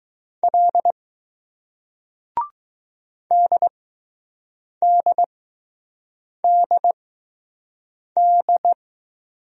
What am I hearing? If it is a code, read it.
Morse code, heard as a keyed tone, about 20 wpm, spaced as for 6 wpm, 700 Hz. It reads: L DDDD